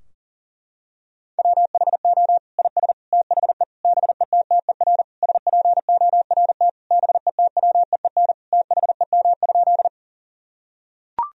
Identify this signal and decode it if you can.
Morse code, heard as a keyed tone, about 37 wpm, 700 Hz. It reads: WHO IS THE BETTER SPORT BETWEEN THEM?
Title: Sentences from Top 1000 Words - Review Part 4 - 40wpm